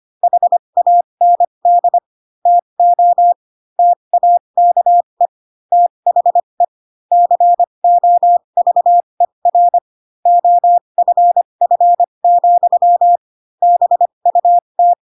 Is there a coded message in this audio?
Yes: Morse code from 0.2 to 14.9 s, heard as a keyed tone, about 25 wpm, spaced as for 22 wpm, 700 Hz. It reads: HAND TO TAKE THE COVER OFF, BUT